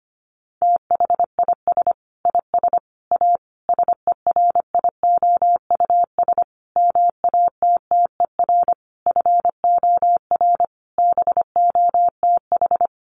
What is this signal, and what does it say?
Morse code, heard as a keyed tone, about 25 wpm, 700 Hz. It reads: THIS IS A SERIOUS MATTER FOR BOTH
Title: Sentences from Top 1100 Words 25wpm